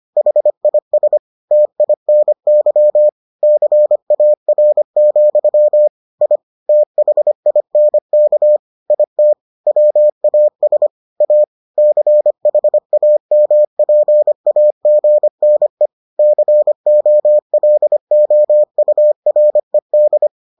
Morse code, heard as a keyed tone, about 25 wpm, 600 Hz. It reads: HIS TINY CAR, I THINK IT WAS A CHAMPAGNE COLOURED